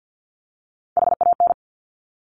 Morse code, heard as a keyed tone, about 50 wpm, 700 Hz. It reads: HAN
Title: Top Three Letter Combinations in Words 50wpm